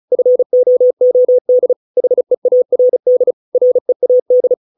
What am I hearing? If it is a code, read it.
Morse code, heard as a keyed tone, about 34 wpm, 500 Hz. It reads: FOOD HEARD READ